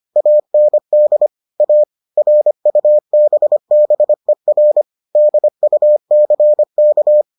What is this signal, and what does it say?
Morse code, heard as a keyed tone, about 25 wpm, 600 Hz. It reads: AND A RUBBER DUCK